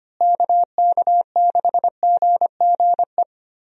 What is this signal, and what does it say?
Morse code, heard as a keyed tone, about 25 wpm, 700 Hz. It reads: KK6GGE
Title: USA Call Signs - 6 Characters - Repeated 2x - 25wpm